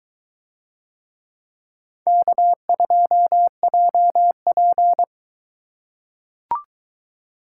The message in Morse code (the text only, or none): K2JP